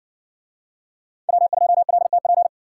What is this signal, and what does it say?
Morse code, heard as a keyed tone, about 60 wpm, 700 Hz. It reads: W2LTP